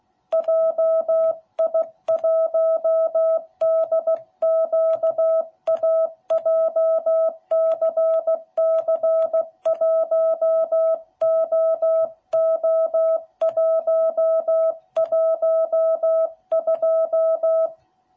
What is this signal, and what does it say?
Morse code, heard as a keyed tone, about 16 wpm, 660 Hz. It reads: JI1DQA JCC1OO112